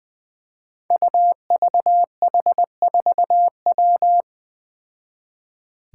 Morse code, heard as a keyed tone, about 20 wpm, 700 Hz. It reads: UVH4W
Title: Five Challenging Letter-Number Combinations 20wpm